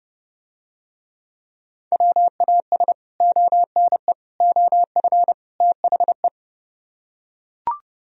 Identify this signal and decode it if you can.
Morse code, heard as a keyed tone, about 30 wpm, 700 Hz. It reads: WAS ONE OF THE